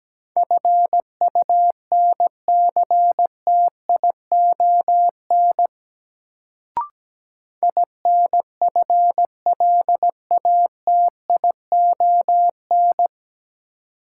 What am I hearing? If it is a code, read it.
Morse code, heard as a keyed tone, about 17 wpm, 700 Hz. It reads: FUNCTION INFLATION
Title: Sentences from Top 1100 Words - New Word Drill Alt - 17wpm